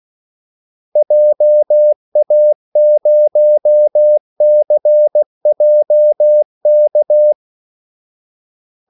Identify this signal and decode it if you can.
Morse code, heard as a keyed tone, about 16 wpm, 600 Hz. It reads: JA0CJK
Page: Kaldesignaler 16 wpm